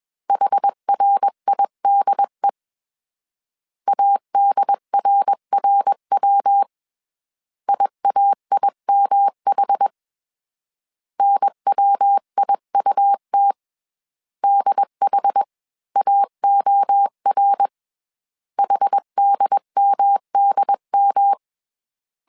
Morse code, heard as a keyed tone, about 21 wpm, 800 Hz. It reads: HRIDE ADRRW IAIMH NWIUT DHAOR HDMDM
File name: CW / DET / 1 3 1500C S B 20wpm 1754